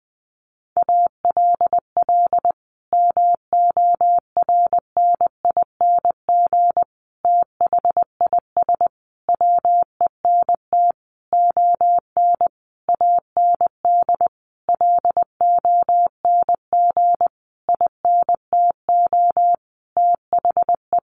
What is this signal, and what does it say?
Morse code, heard as a keyed tone, about 20 wpm, 700 Hz. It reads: ALL MORNING THIS WENT ON AND LONG INTO THE
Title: Sentences from Top 1500 Words 20wpm